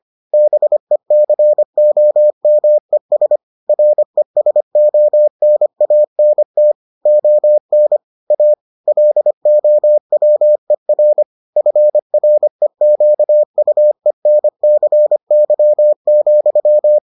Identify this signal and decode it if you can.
Morse code, heard as a keyed tone, about 25 wpm, 600 Hz. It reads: BECOMES RESONANT ON A LOWER FREQUENCY,